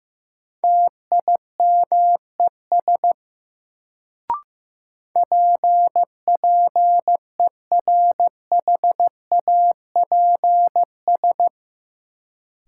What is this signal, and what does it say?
Morse code, heard as a keyed tone, about 15 wpm, 700 Hz. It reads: TIMES PPERHAPS
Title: Sentences from Top 1100 Words - New Word Drill Alt - 15wpm